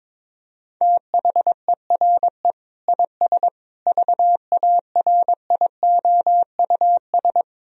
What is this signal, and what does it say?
Morse code, heard as a keyed tone, about 21 wpm, 700 Hz. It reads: THERE IS VARIOUS
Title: Sentences from Top 1000 Words - Review Part 4 - 22wpm